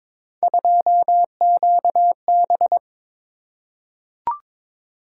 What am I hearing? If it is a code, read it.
Morse code, heard as a keyed tone, about 22 wpm, 700 Hz. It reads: 2QB
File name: practice / 22wpm / Three Letter-Number Combinations - 22wpm